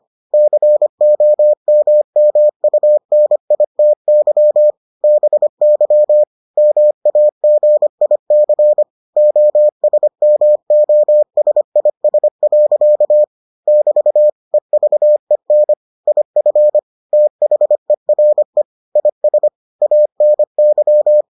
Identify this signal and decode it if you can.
Morse code, heard as a keyed tone, about 25 wpm, 600 Hz. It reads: COMMUNITY BY MAGIC OSMOSIS. <BT> EVEN IF THERE IS ANY